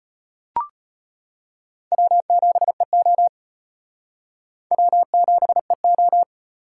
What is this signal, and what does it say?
Morse code, heard as a keyed tone, about 31 wpm, 700 Hz. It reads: W7EO W7EO
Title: Speed Racing - USA 1x2 Call Signs 25wpm